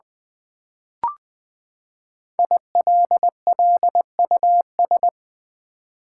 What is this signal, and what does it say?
Morse code, heard as a keyed tone, about 20 wpm, 700 Hz. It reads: ILLUS
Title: Build A Word - 5 Letters At A Time - 10-Letter Words - 20wpm